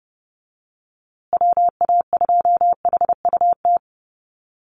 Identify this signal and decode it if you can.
Morse code, heard as a keyed tone, about 30 wpm, 700 Hz. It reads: WA2HUT
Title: USA Call Signs Repeated 2x 30wpm